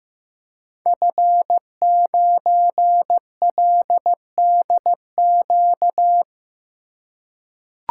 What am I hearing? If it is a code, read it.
Morse code, heard as a keyed tone, about 15 wpm, 700 Hz. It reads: F9LDQ
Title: Code Groups - 5 Letter:Numbers - 15wpm